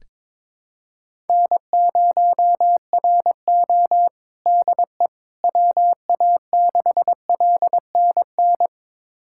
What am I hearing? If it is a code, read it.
Morse code, heard as a keyed tone, about 22 wpm, 700 Hz. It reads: N0RO DE WA6LNN
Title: USA Callsign DE Callsign 22wpm